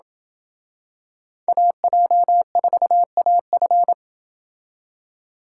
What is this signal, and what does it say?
Morse code, heard as a keyed tone, about 24 wpm, 700 Hz. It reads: AJ4AF
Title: Speed Racing - USA 2x2 Call Signs 20wpm